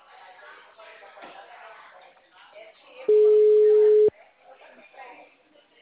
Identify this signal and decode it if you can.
Ringback tone (Europe, 425 Hz), with speech.
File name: out- -2011- - - .5193